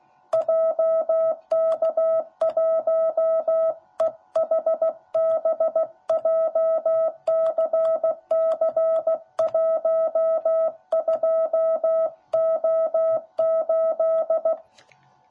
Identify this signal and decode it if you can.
Morse code, heard as a keyed tone, about 16 wpm, 660 Hz. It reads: JK1EHB JCC12O8
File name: jh8cbh / 2021cw / cw10b